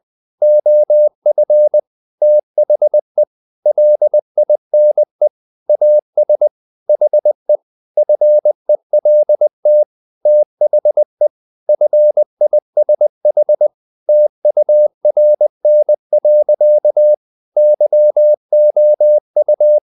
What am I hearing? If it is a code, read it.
Morse code, heard as a keyed tone, about 20 wpm, 600 Hz. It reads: OF THE LINE AS HE FELT THE FISH TURN. YOU
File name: the old man and the sea part 2 20wpm